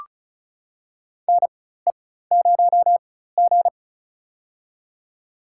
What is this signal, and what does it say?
Morse code, heard as a keyed tone, about 34 wpm, spaced as for 16 wpm, 700 Hz. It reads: NE0G
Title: USA Call Signs - 4 Characters - 4x Character Spacing - 35wpm